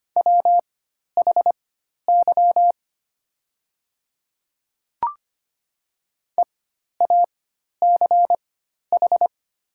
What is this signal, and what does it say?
Morse code, heard as a keyed tone, about 25 wpm, spaced as for 12 wpm, 700 Hz. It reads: WHY EACH
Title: 4x Character Spacing - Top 300 Words - 25wpm